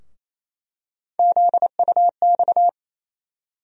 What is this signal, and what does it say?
Morse code, heard as a keyed tone, about 28 wpm, 700 Hz. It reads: ZUX
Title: USA Call Sign Postfixes - 3 Letters 28wpm